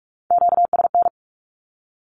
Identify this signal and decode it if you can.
Morse code, heard as a keyed tone, about 46 wpm, 700 Hz. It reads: QSN
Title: Q-Codes 45wpm